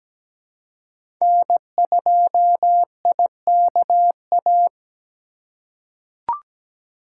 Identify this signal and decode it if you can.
Morse code, heard as a keyed tone, about 17 wpm, 700 Hz. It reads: N2IKA
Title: USA Call Signs - 5 Characters - 2x Character Spacing - 17wpm